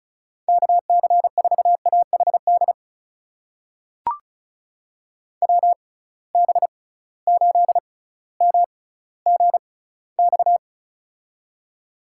Morse code, heard as a keyed tone, about 36 wpm, spaced as for 18 wpm, 700 Hz. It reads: KC4AHD WB8MGX
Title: USA Call Signs - 6 Characters - 6x Character Spacing - 35wpm